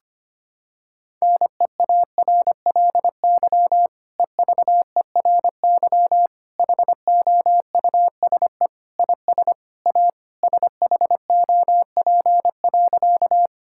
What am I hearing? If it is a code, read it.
Morse code, heard as a keyed tone, about 25 wpm, 700 Hz. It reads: NEARLY EVERY HOUSE IS A SHOP.
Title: Sentences from Top 1600 Words 25wpm